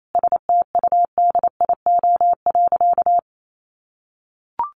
Morse code, heard as a keyed tone, about 28 wpm, 700 Hz. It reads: STUDIO.
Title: Sentences from Top 1200 Words 28wpm